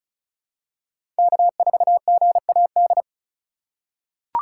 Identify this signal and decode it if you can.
Morse code, heard as a keyed tone, about 36 wpm, 690 Hz. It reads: K4GAD
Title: USA Call Signs - 5 Characters - 3x Character Spacing - 35wpm